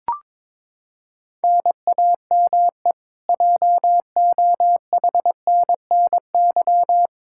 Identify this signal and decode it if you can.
Morse code, heard as a keyed tone, about 22 wpm, 700 Hz. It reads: NAME JOHNNY
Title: Practice QSO - Exchange 1 - Beginning - 22wpm